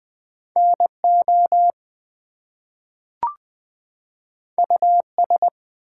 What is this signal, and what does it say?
Morse code, heard as a keyed tone, about 20 wpm, 700 Hz. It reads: NO US
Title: Two Letter Words - 20wpm